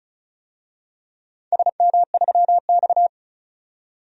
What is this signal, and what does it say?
Morse code, heard as a keyed tone, about 35 wpm, 700 Hz. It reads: SM3X
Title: DXCC-100 Call Signs - 4 Characters - 35wpm